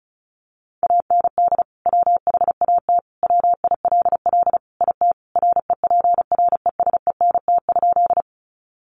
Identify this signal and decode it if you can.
Morse code, heard as a keyed tone, about 34 wpm, 700 Hz. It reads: AND WHAT WILL IT REPRESENT?
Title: Sentences from Top 1200 Words 35wpm